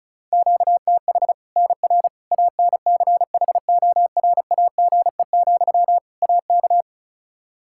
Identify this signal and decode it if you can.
Morse code, heard as a keyed tone, about 34 wpm, 700 Hz. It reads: QTH NR ANCHORAGE, AK
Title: K1USN SST QSO Practice 35wpm